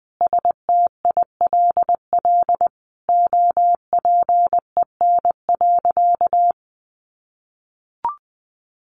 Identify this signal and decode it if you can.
Morse code, heard as a keyed tone, about 20 wpm, 700 Hz. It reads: STILL OPEN.
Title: Sentences from Top 1500 Words 20wpm